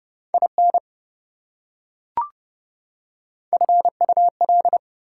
Morse code, heard as a keyed tone, about 28 wpm, 700 Hz. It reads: IN FUL